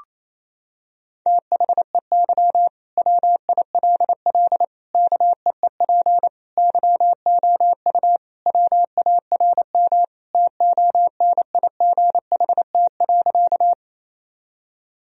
Morse code, heard as a keyed tone, about 28 wpm, 700 Hz. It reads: THEY WILL KEEP YOU WARM TONIGHT.
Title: Sentences from Top 1000 Words - Review Part 3 - 28wpm